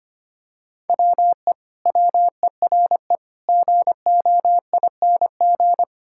Morse code, heard as a keyed tone, about 25 wpm, 700 Hz. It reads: WE WERE GOING
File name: practice / 25wpm / Sentences with 4-Word Phrases - 25wpm